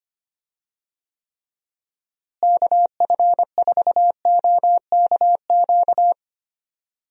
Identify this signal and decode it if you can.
Morse code, heard as a keyed tone, about 25 wpm, 700 Hz. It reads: KF4OKQ
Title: USA Call Signs - 6 Characters - Repeated 3x - 25wpm